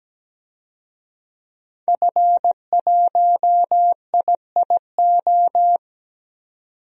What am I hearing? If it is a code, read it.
Morse code, heard as a keyed tone, about 17 wpm, 700 Hz. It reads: F1IIO
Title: DXCC-100 Call Signs - 5 Characters - 17wpm